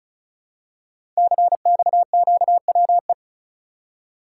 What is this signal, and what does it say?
Morse code, heard as a keyed tone, about 36 wpm, 700 Hz. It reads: CXQWE
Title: Code Groups - 5 Letters - 35wpm